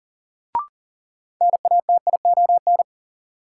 Morse code, heard as a keyed tone, about 38 wpm, 700 Hz. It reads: NATION
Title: Sets of 3 Words - New Word Drill - 40wpm